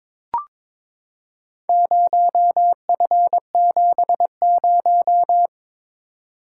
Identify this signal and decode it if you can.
Morse code, heard as a keyed tone, about 22 wpm, 700 Hz. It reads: 0F70
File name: Four Letter-Number Combinations - 22wpm